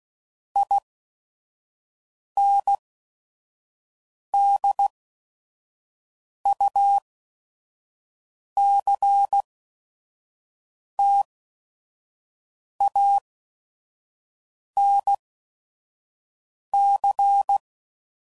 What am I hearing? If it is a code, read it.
Morse code, heard as a keyed tone, about 16 wpm, 800 Hz. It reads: I N D U C T A N C